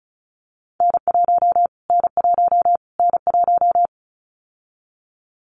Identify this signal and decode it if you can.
Morse code, heard as a keyed tone, about 34 wpm, 700 Hz. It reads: N1 N1 N1
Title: Callsign Prefixes for Top 25 DXCC - Repeated 3x - 35wpm